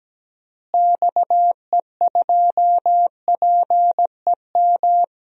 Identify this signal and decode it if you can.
Morse code, heard as a keyed tone, about 17 wpm, 700 Hz. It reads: XE2PEM
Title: DXCC-100 Call Signs - 6 Characters - Repeated 3x - 17wpm